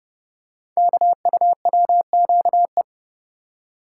Morse code, heard as a keyed tone, about 30 wpm, 700 Hz. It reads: KUWQE
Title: Code Groups - 5 Letters - 30wpm